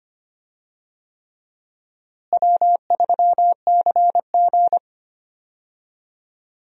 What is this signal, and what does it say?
Morse code, heard as a keyed tone, about 25 wpm, 700 Hz. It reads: W3CG
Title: USA Call Signs - 4 Characters - Repeated 3x - 25wpm